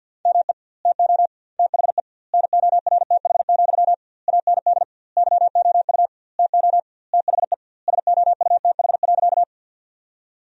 Morse code, heard as a keyed tone, about 50 wpm, 700 Hz. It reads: ME TO THE NORTH, AND YOU TO THE SOUTH.
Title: 2x Word-Spacing - Sentences from Top 400 Words - 50wpm